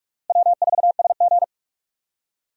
Morse code, heard as a keyed tone, about 46 wpm, 700 Hz. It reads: W4SG
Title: USA Call Signs - 4 Characters - Repeated 2x - 45wpm